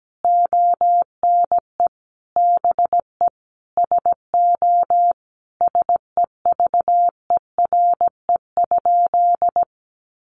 Morse code, heard as a keyed tone, about 17 wpm, 700 Hz. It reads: ONE BE SO SEVERE?
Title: Sentences from Top 1400 Words 17wpm